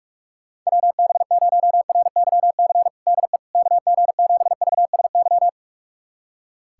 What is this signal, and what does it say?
Morse code, heard as a keyed tone, about 44 wpm, 700 Hz. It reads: WB0RYC DE KG7VSY
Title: USA Callsign DE Callsign 45wpm